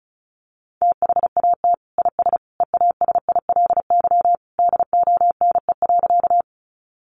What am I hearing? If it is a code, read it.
Morse code, heard as a keyed tone, about 34 wpm, 700 Hz. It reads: THAT IS EASILY DONE.
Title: Sentences from Top 900 Words 35wpm